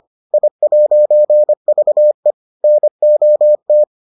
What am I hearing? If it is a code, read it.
Morse code, heard as a keyed tone, about 25 wpm, 600 Hz. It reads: I'VE NOT